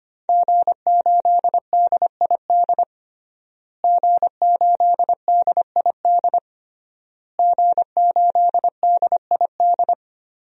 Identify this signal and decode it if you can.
Morse code, heard as a keyed tone, about 25 wpm, 700 Hz. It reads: G8DID G8DID G8DID
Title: DXCC-100 Call Signs - 5 Characters - Repeated 3x - 25wpm